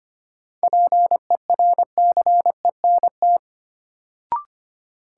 Morse code, heard as a keyed tone, about 25 wpm, 700 Hz. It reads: PERCENT
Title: Sets of 2 Words - New Word Drill - 25wpm